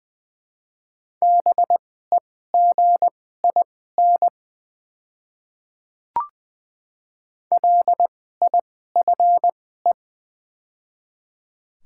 Morse code, heard as a keyed tone, about 21 wpm, spaced as for 15 wpm, 700 Hz. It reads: BEGIN LIFE
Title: 2x Character Spacing - Top 300 Words - 20wpm